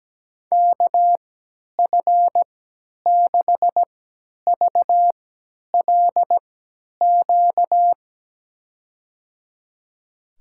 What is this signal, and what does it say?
Morse code, heard as a keyed tone, about 17 wpm, spaced as for 10 wpm, 700 Hz. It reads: KF6VLQ
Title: USA Call Signs - 6 Characters - 3x Character Spacing - 17wpm